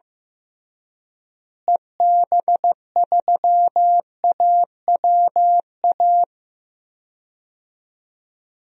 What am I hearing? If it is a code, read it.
Morse code, heard as a keyed tone, about 16 wpm, 700 Hz. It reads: EB3AWA
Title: DXCC-100 Call Signs - 6 Characters - Repeated 2x - 15wpm